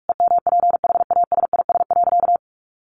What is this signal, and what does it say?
Morse code, heard as a keyed tone, about 44 wpm, 700 Hz. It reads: EMPHASIS.
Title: Sentences from Top 1400 Words 45wpm